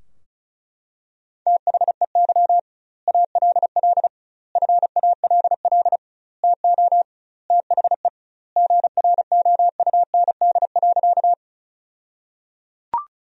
Morse code, heard as a keyed tone, about 34 wpm, 700 Hz. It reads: THEY ALL FALL TO THE GROUND.
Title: 2x Word-Spacing - Sentences from Top 500 Words - 35wpm